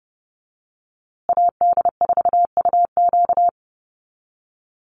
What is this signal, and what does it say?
Morse code, heard as a keyed tone, about 30 wpm, 700 Hz. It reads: AD4UQ
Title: USA Call Signs Repeated 3x 30wpm